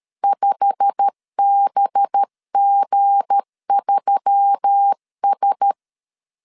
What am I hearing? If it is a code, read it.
Morse code, heard as a keyed tone, about 13 wpm, 800 Hz. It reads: 5BG3S
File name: LEZIONI / 9 28 12wpm 2351